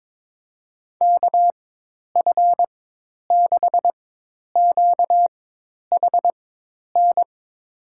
Morse code, heard as a keyed tone, about 21 wpm, spaced as for 10 wpm, 700 Hz. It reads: KF6QHN
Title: USA Call Signs - 6 Characters - 4x Character Spacing - 22wpm